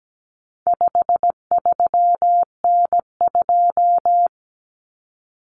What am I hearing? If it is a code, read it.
Morse code, heard as a keyed tone, about 17 wpm, 700 Hz. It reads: 53N2
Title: Common Cut Numbers - 4-Digit 17wpm